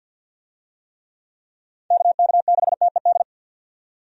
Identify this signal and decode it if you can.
Morse code, heard as a keyed tone, about 46 wpm, 700 Hz. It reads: KK6TED
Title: USA Call Signs - 6 Characters - 50wpm